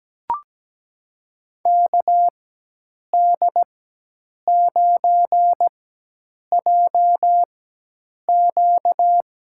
Morse code, heard as a keyed tone, about 17 wpm, spaced as for 8 wpm, 700 Hz. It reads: KD9JQ
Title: USA Call Signs - 4x Character Spacing - 17wpm